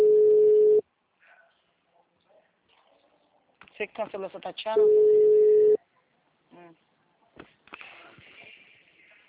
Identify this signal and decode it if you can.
Ringback tone (Europe, 425 Hz), with speech.